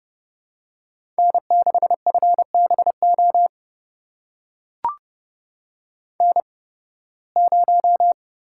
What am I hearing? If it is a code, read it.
Morse code, heard as a keyed tone, about 30 wpm, spaced as for 20 wpm, 700 Hz. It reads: N6FBO N0
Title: USA Call Signs - 5 Characters - 8x Character Spacing - 30wpm